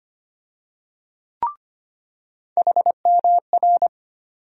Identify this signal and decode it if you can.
Morse code, heard as a keyed tone, about 25 wpm, 700 Hz. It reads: HMR
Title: USA Call Sign Postfixes - 3 Letters 25wpm